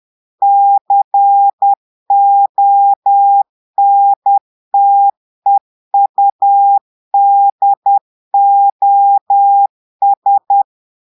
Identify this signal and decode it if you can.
Morse code, heard as a keyed tone, about 10 wpm, 800 Hz. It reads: CONTEUDOS
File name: portugais texte en clair 2 vitesse 10 mots minutes